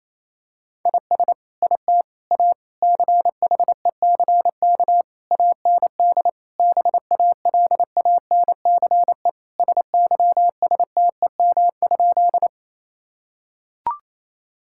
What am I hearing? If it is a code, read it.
Morse code, heard as a keyed tone, about 28 wpm, 700 Hz. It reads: IS IT A CHECK AND BALANCE SYSTEM?
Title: Sentences from Top 1000 Words - Review Part 6 - 28wpm